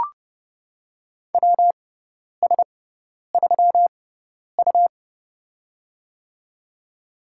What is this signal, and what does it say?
Morse code, heard as a keyed tone, about 30 wpm, spaced as for 10 wpm, 700 Hz. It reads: WS3U
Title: USA Call Signs - 4 Characters - 6x Character Spacing - 30wpm